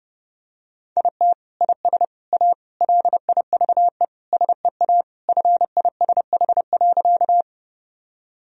Morse code, heard as a keyed tone, about 32 wpm, 700 Hz. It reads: IT IS A LIVE SEA FISH.
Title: Sentences from Top 1000 Words - Review Part 1 - 30wpm